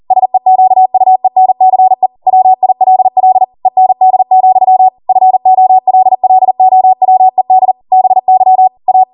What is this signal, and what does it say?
Morse code, heard as a keyed tone, about 40 wpm, 750 Hz. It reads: SEQUENCE WILL END, FOLLOWED BY A